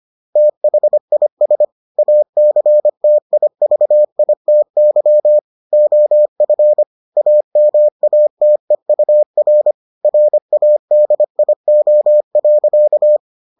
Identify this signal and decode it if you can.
Morse code, heard as a keyed tone, about 25 wpm, 600 Hz. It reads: THIS ACTIVITY OF AMATEUR RADIO.